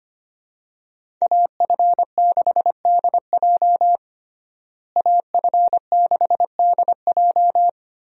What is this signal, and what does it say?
Morse code, heard as a keyed tone, about 25 wpm, 700 Hz. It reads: AF6DJ AF6DJ
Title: USA Call Signs - 5 Characters - Repeated 3x - 25wpm